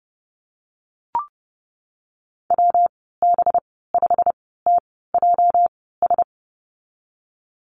Morse code, heard as a keyed tone, about 30 wpm, spaced as for 17 wpm, 700 Hz. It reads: WB5TJS
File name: Lesson 37g - Letter x - Call Signs 30wpm